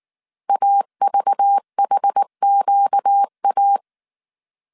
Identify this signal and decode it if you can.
Morse code, heard as a keyed tone, about 19 wpm, 800 Hz. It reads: AVHQA
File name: CW / DET / 1 9 1500C S A 18wpm 2459